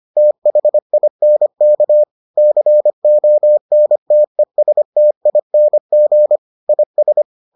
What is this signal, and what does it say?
Morse code, heard as a keyed tone, about 25 wpm, 600 Hz. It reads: THINK CONTESTING IS